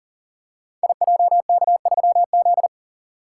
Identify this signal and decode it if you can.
Morse code, heard as a keyed tone, about 40 wpm, 690 Hz. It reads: IJK3Z
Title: Code Groups - 5 Letter:Numbers - 40wpm